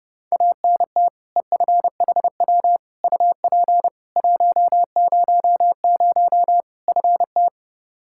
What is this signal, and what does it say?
Morse code, heard as a keyed tone, about 30 wpm, 700 Hz. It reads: ANT EFHW UP 100 FT
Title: Practice QSO - Exchange 2 - Info - 30wpm